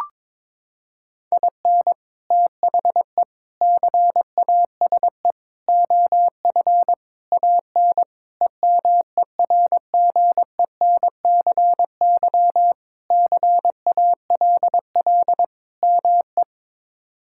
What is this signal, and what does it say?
Morse code, heard as a keyed tone, about 22 wpm, 700 Hz. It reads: IN THE CASE OF AN EMERGENCY CALL ME
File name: practice / 22wpm / Sentences with 3-Word Phrases - 22wpm